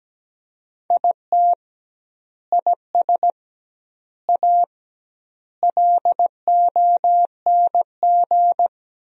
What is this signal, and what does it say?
Morse code, heard as a keyed tone, about 17 wpm, 700 Hz. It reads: IT IS A LONG
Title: 2x Word-Spacing - Sentences from Top 300 Words - 17wpm